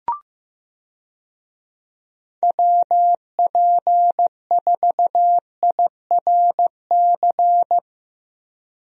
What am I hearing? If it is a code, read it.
Morse code, heard as a keyed tone, about 15 wpm, 700 Hz. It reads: WP4IRC
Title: USA Call Signs - 6 Characters - Repeated 3x - 15wpm